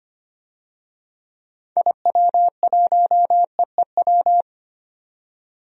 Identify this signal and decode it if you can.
Morse code, heard as a keyed tone, about 25 wpm, 700 Hz. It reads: IW1EEW
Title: DXCC-100 Call Signs - 6 Characters - 25wpm